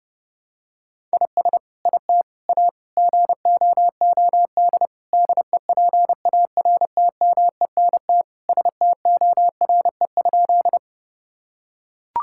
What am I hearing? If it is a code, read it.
Morse code, heard as a keyed tone, about 30 wpm, 700 Hz. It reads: IS IT A GOOD DEPARTMENT STORE?
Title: Sentences from Top 1000 Words - Review Part 5 - 30wpm